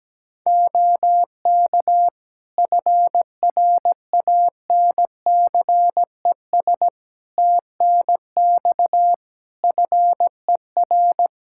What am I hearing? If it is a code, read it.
Morse code, heard as a keyed tone, about 17 wpm, 700 Hz. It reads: OK FRANCES TNX FER
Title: Practice QSO - Exchange 4 - Ending - 17wpm